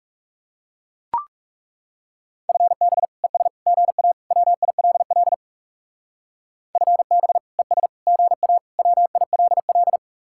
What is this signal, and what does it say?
Morse code, heard as a keyed tone, about 40 wpm, 700 Hz. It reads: FB ES GA WILL FB ES GA WILL